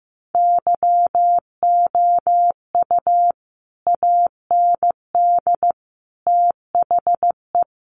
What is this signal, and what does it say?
Morse code, heard as a keyed tone, about 15 wpm, 700 Hz. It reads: YOU AND THE
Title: Sentences from Top 1300 Words 15wpm